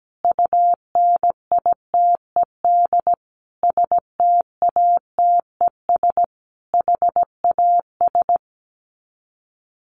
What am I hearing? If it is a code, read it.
Morse code, heard as a keyed tone, about 17 wpm, 700 Hz. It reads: UNITED STATES HAS
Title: Sets of 5 Words Encore 17wpm